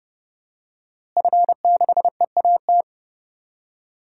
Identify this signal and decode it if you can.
Morse code, heard as a keyed tone, about 31 wpm, 700 Hz. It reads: F6EAT
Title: DXCC-100 Call Signs - 5 Characters - Repeated 3x - 30wpm